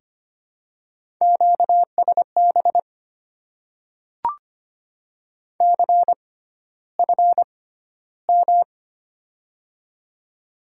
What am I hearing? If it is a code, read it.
Morse code, heard as a keyed tone, about 25 wpm, spaced as for 13 wpm, 700 Hz. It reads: QSB CFM
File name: Top 90 QSO Elements - 6x Character Spacing - 25wpm